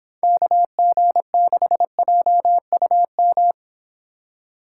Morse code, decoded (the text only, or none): KG6JUM